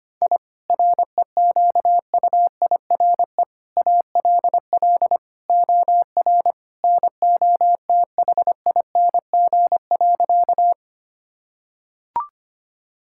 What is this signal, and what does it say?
Morse code, heard as a keyed tone, about 25 wpm, 700 Hz. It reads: I REQUIRE ALL OR NOTHING.
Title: Sentences from Top 1000 Words - Review Part 6 - 25wpm